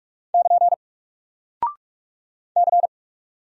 Morse code, heard as a keyed tone, about 45 wpm, 700 Hz. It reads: <KN> C
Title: Single Character (Letter:Number:Punctuation:Prosign) Alt 45wpm